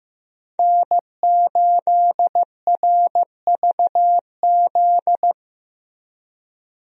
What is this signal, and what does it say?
Morse code, heard as a keyed tone, about 15 wpm, 700 Hz. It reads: N8RVZ
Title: USA Call Signs - 5 Characters - Repeated 2x - 15wpm